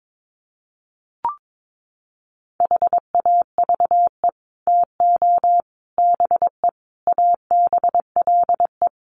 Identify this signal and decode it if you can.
Morse code, heard as a keyed tone, about 22 wpm, 700 Hz. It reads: HAVE TO BE ABLE
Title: Sets of 5 Words 22wpm